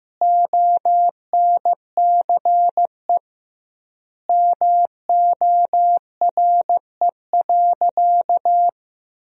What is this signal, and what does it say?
Morse code, heard as a keyed tone, about 15 wpm, 700 Hz. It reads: ONCE MORE.
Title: 2x Word-Spacing - Sentences from Top 500 Words - 15wpm